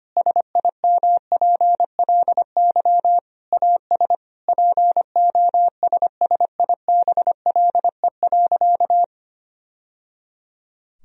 Morse code, heard as a keyed tone, about 25 wpm, 700 Hz. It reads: SIMPLY AS POSSIBLE.
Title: Sentences from Top 1000 Words - Review Part 4 - 25wpm